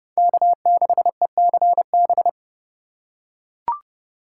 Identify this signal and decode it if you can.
Morse code, heard as a keyed tone, about 29 wpm, 700 Hz. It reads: K6ECB